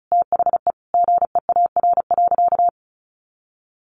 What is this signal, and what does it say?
Morse code, heard as a keyed tone, about 34 wpm, 700 Hz. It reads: THE GEAR.
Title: Sentences from Top 1400 Words 35wpm